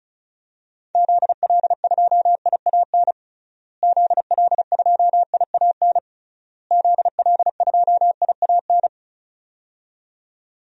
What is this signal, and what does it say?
Morse code, heard as a keyed tone, about 34 wpm, 700 Hz. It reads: ZL2IAN ZL2IAN ZL2IAN
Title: DXCC-100 Call Signs - 6 Characters - Repeated 3x - 35wpm